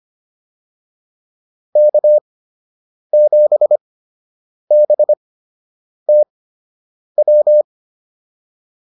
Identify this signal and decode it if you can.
Morse code, heard as a keyed tone, about 25 wpm, spaced as for 8 wpm, 600 Hz. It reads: K7BTW